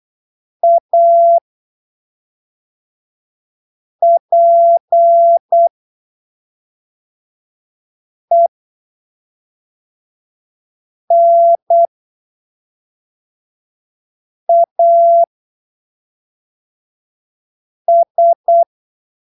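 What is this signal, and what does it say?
Morse code, heard as a keyed tone, about 8 wpm, spaced as for 3 wpm, 680 Hz. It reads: APENAS
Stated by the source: Morse training text at 8 ppm for Brazilian class B amateur radio telegraphy exams, from Portal do CW, PY2BRL.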